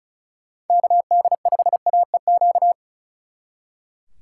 Morse code, heard as a keyed tone, about 46 wpm, 700 Hz. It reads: KD5AEQ